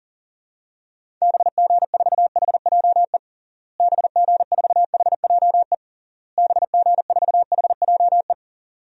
Morse code, heard as a keyed tone, about 37 wpm, 700 Hz. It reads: BG4HJE BG4HJE BG4HJE
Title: DXCC-100 Call Signs - 6 Characters - Repeated 3x - 40wpm